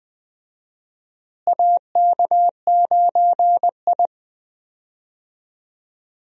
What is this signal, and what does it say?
Morse code, heard as a keyed tone, about 20 wpm, 700 Hz. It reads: AK9I
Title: USA Call Signs - 4 Characters - Repeated 3x - 20wpm